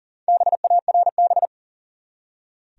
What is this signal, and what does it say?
Morse code, heard as a keyed tone, about 38 wpm, 700 Hz. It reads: BARB
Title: USA Common Names - Max 4 Letters - 40wpm